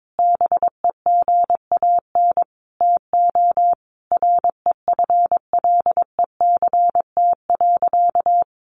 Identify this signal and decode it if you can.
Morse code, heard as a keyed tone, about 22 wpm, 700 Hz. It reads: BEGAN TO REFLECT.
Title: Sentences from Top 1500 Words 22wpm